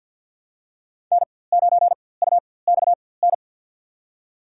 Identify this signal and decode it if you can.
Morse code, heard as a keyed tone, about 53 wpm, spaced as for 24 wpm, 700 Hz. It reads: N9UXN
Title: USA Call Signs - 5 Characters - 4x Character Spacing - 50wpm